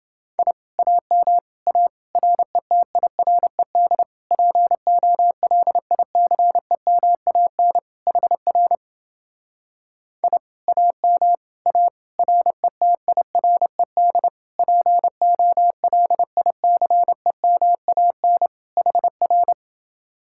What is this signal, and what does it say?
Morse code, heard as a keyed tone, about 28 wpm, 700 Hz. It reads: I AM A RETIRED POLICEMAN HR I AM A RETIRED POLICEMAN HR